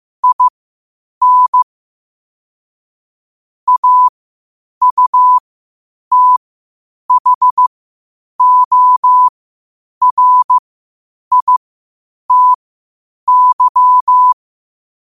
Morse code, heard as a keyed tone, about 15 wpm, spaced as for 8 wpm, 1000 Hz. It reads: IN AUTHORITY
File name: qotdfiles / 8wpm